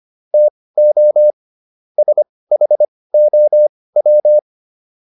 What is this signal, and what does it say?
Morse code, heard as a keyed tone, about 25 wpm, spaced as for 18 wpm, 600 Hz. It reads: TO SHOW